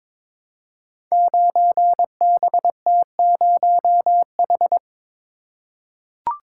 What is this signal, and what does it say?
Morse code, heard as a keyed tone, about 22 wpm, 700 Hz. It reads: 9BT0H
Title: Code Groups - 5 Letter:Numbers - 22wpm